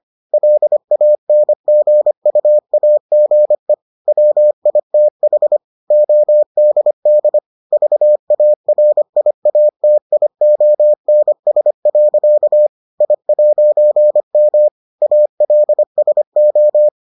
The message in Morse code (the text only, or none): LANGUAGE WITH ODD VARIATIONS. I'M ALSO